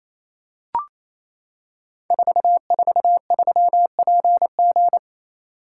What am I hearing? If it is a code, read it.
Morse code, heard as a keyed tone, about 27 wpm, 700 Hz. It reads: 443PG
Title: Code Groups - 5 Letter:Numbers - 28wpm